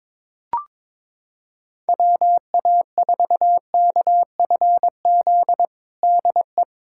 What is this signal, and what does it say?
Morse code, heard as a keyed tone, about 22 wpm, 700 Hz. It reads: WA4KFZ DE
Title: USA Callsign DE Callsign - 6 Characters - 22wpm